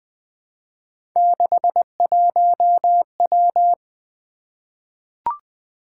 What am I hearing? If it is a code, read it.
Morse code, heard as a keyed tone, about 20 wpm, 700 Hz. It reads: 61W
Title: Three Letter-Number Combinations - 20wpm